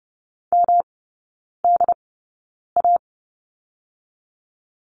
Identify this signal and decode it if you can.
Morse code, heard as a keyed tone, about 30 wpm, spaced as for 9 wpm, 700 Hz. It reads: MDA
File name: Lesson 18e - Letter m - Three Characters 30wpm